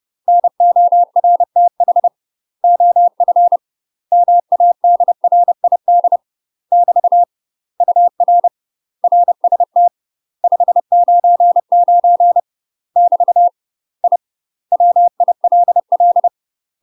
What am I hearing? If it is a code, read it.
Morse code, heard as a keyed tone, about 30 wpm, 700 Hz. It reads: NORTH OF MADRID <BT> UR RST 599 <BT> I WILL
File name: UK MASTER QSOs TRAINING / 30WPM / UK MASTER QSOs 0039